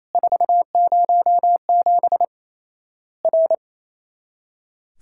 Morse code, heard as a keyed tone, about 28 wpm, 700 Hz. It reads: 407 R
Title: Worked All Europe DX Contest - QTC Exchange - 28wpm